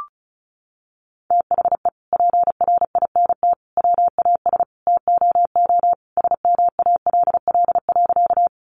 Morse code, heard as a keyed tone, about 34 wpm, 700 Hz. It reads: THE PRINT WAS TOO SMALL.
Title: Sentences from Top 1400 Words 35wpm